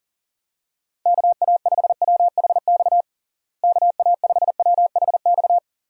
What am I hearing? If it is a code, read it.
Morse code, heard as a keyed tone, about 40 wpm, 700 Hz. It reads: KA5WHX KA5WHX
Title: USA Call Signs - 6 Characters - Repeated 3x - 40wpm